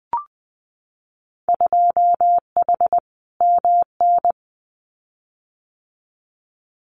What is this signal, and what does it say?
Morse code, heard as a keyed tone, about 20 wpm, 700 Hz. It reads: 2H MN
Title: Winter Field Day Exchanges 20wpm